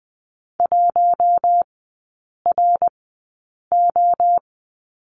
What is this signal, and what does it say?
Morse code, heard as a keyed tone, about 20 wpm, spaced as for 8 wpm, 700 Hz. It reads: 1RO
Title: 087 Lesson 15e - Number 5 - Three Characters 20wpm